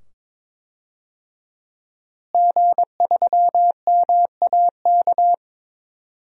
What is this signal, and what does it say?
Morse code, heard as a keyed tone, about 22 wpm, 700 Hz. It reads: G3MAK
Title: DXCC-100 Call Signs - 5 Characters - Repeated 2x - 22wpm